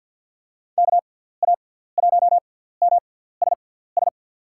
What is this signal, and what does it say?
Morse code, heard as a keyed tone, about 47 wpm, spaced as for 17 wpm, 690 Hz. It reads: KA1MSS
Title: USA Call Signs - 6x Character Spacing - 50wpm